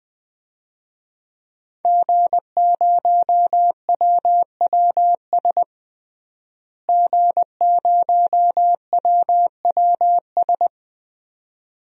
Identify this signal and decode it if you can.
Morse code, heard as a keyed tone, about 20 wpm, 700 Hz. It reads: G0WWS G0WWS
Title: DXCC-100 Call Signs - 5 Characters - Repeated 2x - 20wpm